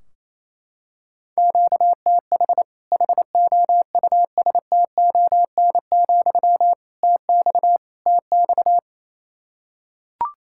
Morse code, heard as a keyed tone, about 28 wpm, 700 Hz. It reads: QTH HOUSTON, TX TX
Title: Practice QSO - Exchange 1 - Beginning - 28wpm